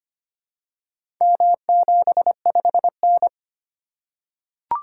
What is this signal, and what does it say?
Morse code, heard as a keyed tone, about 25 wpm, 700 Hz. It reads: M75N
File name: practice / 25wpm / Four Letter-Number Combinations - 25wpm